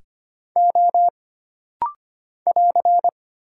Morse code, heard as a keyed tone, about 25 wpm, 700 Hz. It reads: O <AR>
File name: Single Character (Letter:Number:Punctuation:Prosign) - Rapid-Fire - 25wpm